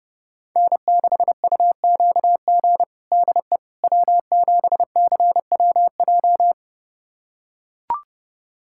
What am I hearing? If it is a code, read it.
Morse code, heard as a keyed tone, about 30 wpm, 700 Hz. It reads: N6UQG DE W7CWJ
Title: USA Callsign DE Callsign - 5 Characters - 30wpm